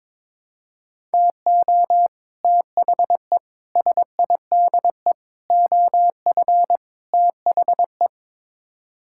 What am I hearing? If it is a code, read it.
Morse code, heard as a keyed tone, about 22 wpm, 700 Hz. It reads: TO THE SIDE OF THE
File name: practice / 22wpm / Build A Phrase - 5-Word Phrases - 22wpm